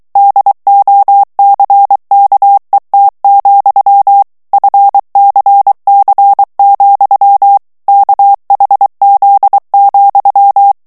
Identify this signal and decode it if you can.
Morse code, heard as a keyed tone, about 23 wpm, 800 Hz. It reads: DOCKET, FCC, KHZ,
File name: ANGLAIS texte clair 3 vitesse 22 mots minute